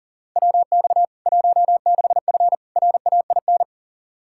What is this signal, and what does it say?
Morse code, heard as a keyed tone, about 40 wpm, 700 Hz. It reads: WX 16F RAIN
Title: K1USN SST QSO Practice 40wpm